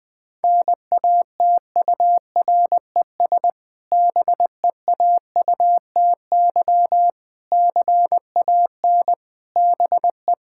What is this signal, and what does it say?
Morse code, heard as a keyed tone, about 20 wpm, 700 Hz. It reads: NATURES BEAUTY CAN BE
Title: Difficult Sentences with Hints - 20wpm